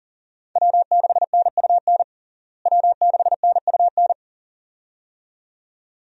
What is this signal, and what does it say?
Morse code, heard as a keyed tone, about 39 wpm, 700 Hz. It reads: W6NUN W6NUN
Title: USA Call Signs - 5 Characters - Repeated 2x - 40wpm